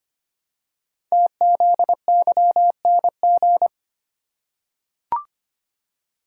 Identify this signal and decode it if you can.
Morse code, heard as a keyed tone, about 25 wpm, 700 Hz. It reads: TZYNG
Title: Code Groups - 5 Letter:Numbers - 25wpm